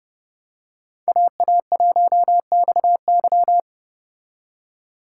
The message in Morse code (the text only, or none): AA1XY